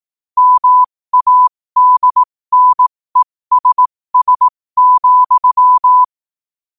Morse code, heard as a keyed tone, about 18 wpm, spaced as for 15 wpm, 1000 Hz. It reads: MADNESS,
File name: qotdfiles / 15wpm